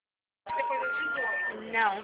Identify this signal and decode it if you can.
Special information tone, with speech.